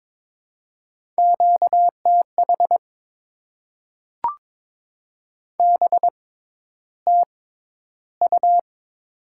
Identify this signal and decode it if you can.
Morse code, heard as a keyed tone, about 22 wpm, spaced as for 11 wpm, 700 Hz. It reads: QTH BTU
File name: Top 20 QSO Elements - 6x Character Spacing - 22wpm